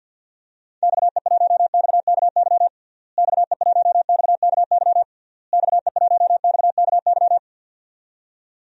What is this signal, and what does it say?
Morse code, heard as a keyed tone, about 50 wpm, 700 Hz. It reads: XE1XKY XE1XKY XE1XKY
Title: DXCC-100 Call Signs - 6 Characters - Repeated 3x - 50wpm